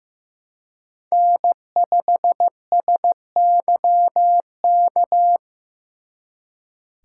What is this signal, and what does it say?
Morse code, heard as a keyed tone, about 18 wpm, 700 Hz. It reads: N5SYK
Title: Speed Racing - USA 1x3 Call Signs 15wpm